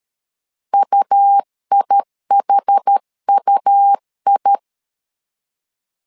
Morse code, heard as a keyed tone, about 12 wpm, 800 Hz. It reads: UIHUI
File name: LEZIONI / 2 UHDN / 2 UH 12wpm 1439